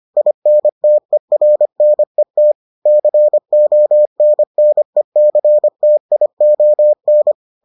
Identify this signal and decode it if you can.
Morse code, heard as a keyed tone, about 25 wpm, 600 Hz. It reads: INTERNET CONNECTION